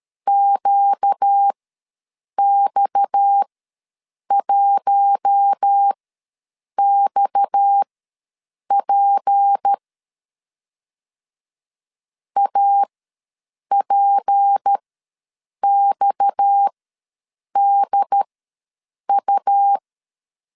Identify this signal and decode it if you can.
Morse code, heard as a keyed tone, about 13 wpm, spaced as for 7 wpm, 800 Hz. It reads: QX1XP APXDU
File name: LEZIONI / 6 PKX0 / 6 PKX ws 12wpm 3106